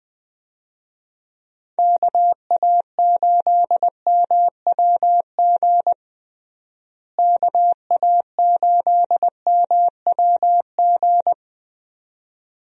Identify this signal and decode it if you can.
Morse code, heard as a keyed tone, about 20 wpm, 700 Hz. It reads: KA8MWG KA8MWG
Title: USA Call Signs - 6 Characters - Repeated 2x - 20wpm